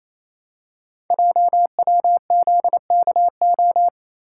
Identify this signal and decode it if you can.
Morse code, heard as a keyed tone, about 27 wpm, 700 Hz. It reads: JWZKO
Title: Code Groups - 5 Letters - 28wpm